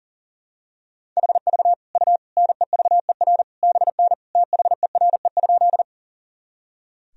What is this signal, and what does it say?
Morse code, heard as a keyed tone, about 37 wpm, 700 Hz. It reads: HV U NEVER BN THERE?
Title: Sentences from Top 300 Words and Abbreviations 40wpm